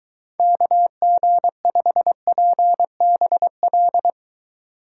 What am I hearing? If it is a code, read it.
Morse code, heard as a keyed tone, about 23 wpm, 700 Hz. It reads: KG5PBL